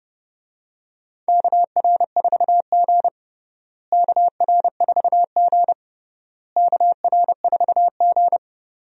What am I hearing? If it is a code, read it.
Morse code, heard as a keyed tone, about 31 wpm, 700 Hz. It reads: KR4G KR4G KR4G
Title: USA Call Signs - 4 Characters - Repeated 3x - 30wpm